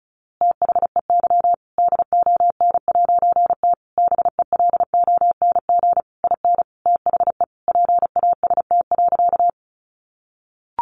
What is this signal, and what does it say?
Morse code, heard as a keyed tone, about 34 wpm, 700 Hz. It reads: THEY DON'T BELONG IN THE PAST.
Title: A Sound Of Thunder 35wpm